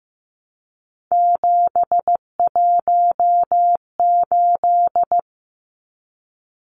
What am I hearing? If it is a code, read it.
Morse code, heard as a keyed tone, about 15 wpm, 700 Hz. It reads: 718
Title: Numbers - 3-Digit 15wpm